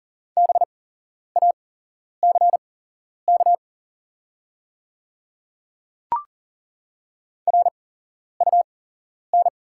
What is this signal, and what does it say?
Morse code, heard as a keyed tone, about 39 wpm, spaced as for 11 wpm, 690 Hz. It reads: BACK RUN
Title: 8x Character Spacing - Top 300 Words - 40wpm